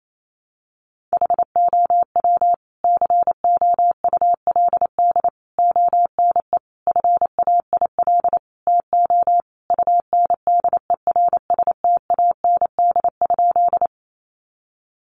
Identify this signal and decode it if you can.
Morse code, heard as a keyed tone, about 28 wpm, 700 Hz. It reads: HOW COULD ONE FAIL TO UNDERSTAND?
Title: Sentences from Top 1100 Words 28wpm